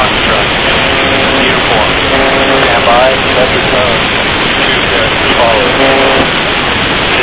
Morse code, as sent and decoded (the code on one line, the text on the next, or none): none